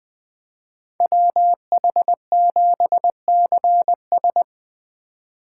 Morse code, heard as a keyed tone, about 25 wpm, 700 Hz. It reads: WH7CS
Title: Speed Racing - USA 2x2 Call Signs 20wpm